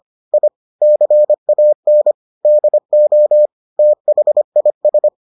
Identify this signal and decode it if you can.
Morse code, heard as a keyed tone, about 25 wpm, 600 Hz. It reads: I CAN DO THIS